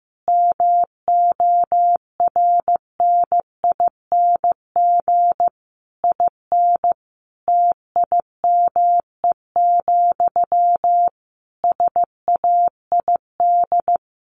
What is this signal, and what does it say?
Morse code, heard as a keyed tone, about 15 wpm, 700 Hz. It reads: MORNING IN TIME, SAID
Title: A Sound Of Thunder 15wpm